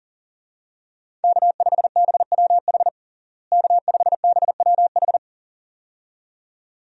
Morse code, heard as a keyed tone, about 37 wpm, 700 Hz. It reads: K5BWH K5BWH
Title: USA Call Signs - 5 Characters - Repeated 2x - 40wpm